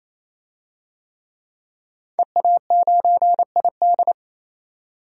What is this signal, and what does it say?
Morse code, heard as a keyed tone, about 28 wpm, 700 Hz. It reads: EA9ID
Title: DXCC-100 Call Signs - 5 Characters - 28wpm